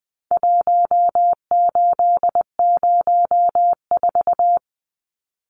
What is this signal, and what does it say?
Morse code, heard as a keyed tone, about 20 wpm, 700 Hz. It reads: 1804
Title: Numbers - Contest Style - 4-Digit 20wpm